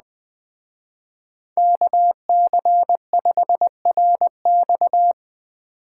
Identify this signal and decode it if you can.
Morse code, heard as a keyed tone, about 20 wpm, 700 Hz. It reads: KC5RX
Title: Speed Racing - USA 2x2 Call Signs 17wpm